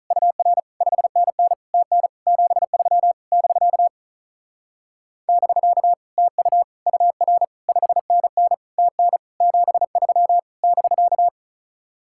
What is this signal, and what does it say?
Morse code, heard as a keyed tone, about 45 wpm, 700 Hz. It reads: UR 5NN TN 73 <BK> <BK> TU UR 5NN TN 73 <BK>